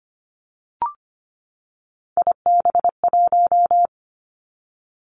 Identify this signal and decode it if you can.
Morse code, heard as a keyed tone, about 25 wpm, 700 Hz. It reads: IB1
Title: Three Challenging Letter-Number Combinations 25wpm